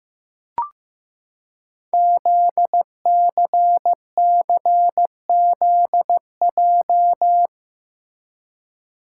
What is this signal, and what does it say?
Morse code, heard as a keyed tone, about 15 wpm, 700 Hz. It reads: ZCCZJ
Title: Code Groups - 5 Letters - 15wpm